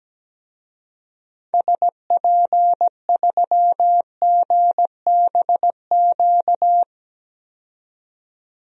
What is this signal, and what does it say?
Morse code, heard as a keyed tone, about 17 wpm, 700 Hz. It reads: SP3GBQ
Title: DXCC-100 Call Signs - 6 Characters - 17wpm